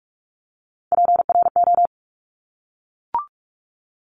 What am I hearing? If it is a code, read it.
Morse code, heard as a keyed tone, about 46 wpm, 700 Hz. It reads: PRO
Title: Top Three Letter Combinations in Words 45wpm